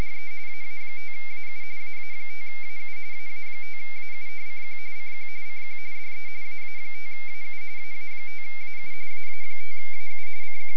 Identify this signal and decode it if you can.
Two-tone teleprinter signal, about 170 Hz shift.